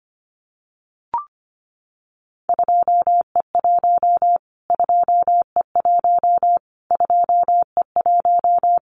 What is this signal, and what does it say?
Morse code, heard as a keyed tone, about 25 wpm, 700 Hz. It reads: 2E1 2E1 2E1
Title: Callsign Prefixes for Top 25 DXCC - Repeated 3x - 25wpm